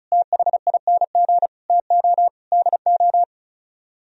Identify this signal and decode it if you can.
Morse code, heard as a keyed tone, about 43 wpm, 700 Hz. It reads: THING TO DO
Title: Speed-Racing - Sets of 5 Words 35wpm